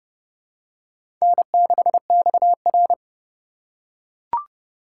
Morse code, heard as a keyed tone, about 30 wpm, 700 Hz. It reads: N6XR
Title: USA Call Signs - 4 Characters - 3x Character Spacing - 30wpm